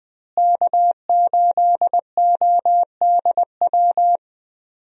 Morse code, heard as a keyed tone, about 20 wpm, 700 Hz. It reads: K8ODW
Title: USA Call Signs - 5 Characters - Repeated 3x - 20wpm